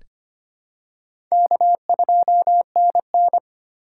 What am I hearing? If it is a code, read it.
Morse code, heard as a keyed tone, about 25 wpm, 700 Hz. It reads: K2NN
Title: USA Call Signs - 4 Characters - 4x Character Spacing - 25wpm